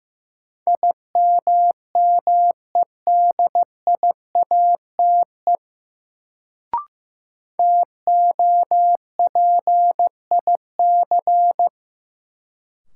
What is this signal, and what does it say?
Morse code, heard as a keyed tone, about 15 wpm, 700 Hz. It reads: IMMEDIATE TOPIC
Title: Sentences from Top 1200 Words - New Word Drill Alt - 15wpm